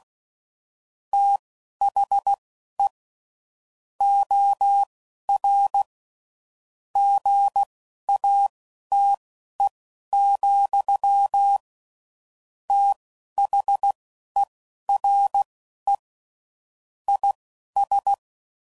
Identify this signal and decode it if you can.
Morse code, heard as a keyed tone, about 16 wpm, spaced as for 11 wpm, 800 Hz. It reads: THE OR GATE, THERE IS
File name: CW Code Groups @10WPM / Technical Subjects / OR Gate